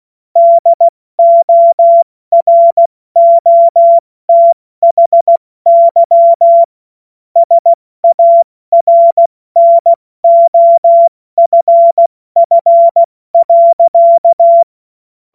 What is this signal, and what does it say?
Morse code, heard as a keyed tone, about 16 wpm, 680 Hz. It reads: DOROTHY SARNOFF.
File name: Morsekurs / Witze / Witz 27. September 2022 70 F80